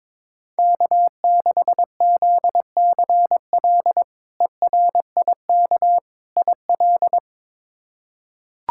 Morse code, heard as a keyed tone, about 22 wpm, 700 Hz. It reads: K6ZCL ERIK IL
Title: K1USN SST QSO Practice - 22wpm